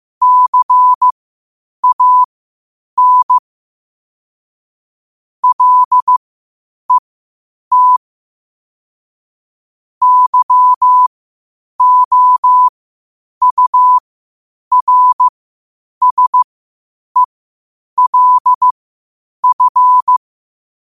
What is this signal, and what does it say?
Morse code, heard as a keyed tone, about 15 wpm, spaced as for 8 wpm, 1000 Hz. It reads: CAN LET YOURSELF
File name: qotdfiles / 8wpm